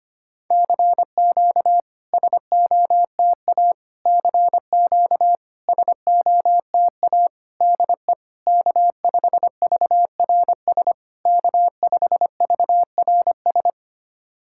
Morse code, heard as a keyed tone, about 25 wpm, 690 Hz. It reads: CQ SOTA CQ SOTA DE K5VRS K5VRS
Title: SOTA QSO Practice 25wpm